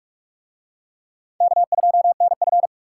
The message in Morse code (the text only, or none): K2NF